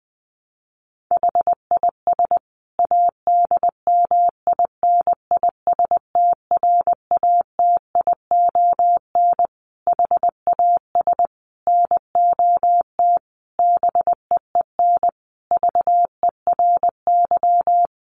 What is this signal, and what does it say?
Morse code, heard as a keyed tone, about 20 wpm, 700 Hz. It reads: HIS ADMINISTRATION HAS NOT BEEN VERY
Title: Sentences from Top 1300 Words 20wpm